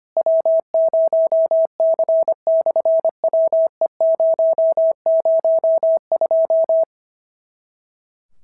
Morse code, heard as a keyed tone, about 25 wpm, 650 Hz. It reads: W0C/WE002
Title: SOTA QSO Practice 25wpm